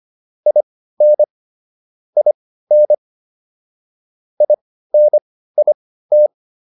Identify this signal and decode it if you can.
Morse code, heard as a keyed tone, about 25 wpm, spaced as for 15 wpm, 600 Hz. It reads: IN IN INIT